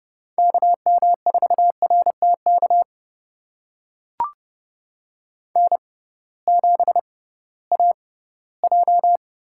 Morse code, heard as a keyed tone, about 32 wpm, spaced as for 18 wpm, 700 Hz. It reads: KM4RTK N7AJ
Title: USA Call Signs - 6x Character Spacing - 30wpm